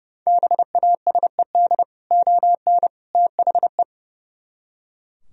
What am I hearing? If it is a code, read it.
Morse code, heard as a keyed tone, about 38 wpm, 700 Hz. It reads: BASED ON THE